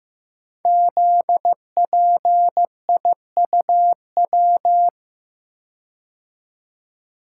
Morse code, heard as a keyed tone, about 15 wpm, 700 Hz. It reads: ZPIUW
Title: Code Groups - 5 Letters - 15wpm